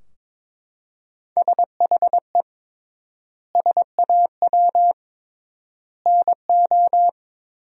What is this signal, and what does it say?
Morse code, heard as a keyed tone, about 22 wpm, 700 Hz. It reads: SHE SAW NO
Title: 3x Word-Spacing - Sentences from Top 300 Words - 22wpm